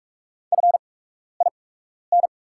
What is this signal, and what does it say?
Morse code, heard as a keyed tone, about 46 wpm, spaced as for 12 wpm, 700 Hz. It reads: FIN
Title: 8x Character Spacing - Top 100 Words - 45wpm